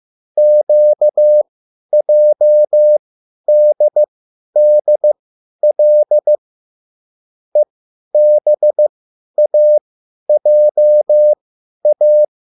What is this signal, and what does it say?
Morse code, heard as a keyed tone, about 15 wpm, spaced as for 10 wpm, 600 Hz. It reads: QJDDL EBAJA
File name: QSL / CW lesson